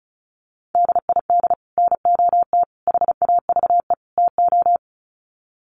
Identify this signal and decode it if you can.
Morse code, heard as a keyed tone, about 34 wpm, 700 Hz. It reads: DID NOT HAVE TO
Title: Sets of 4 Words 35wpm